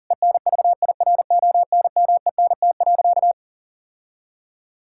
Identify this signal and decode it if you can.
Morse code, heard as a keyed tone, about 40 wpm, 700 Hz. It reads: ENVIRONMENT.
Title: Sentences from Top 1000 Words - Review Part 5 - 40wpm